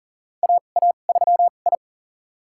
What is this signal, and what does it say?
Morse code, heard as a keyed tone, about 40 wpm, spaced as for 29 wpm, 700 Hz. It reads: AA3I
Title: USA Call Signs - 4 Characters - 2x Character Spacing - 40wpm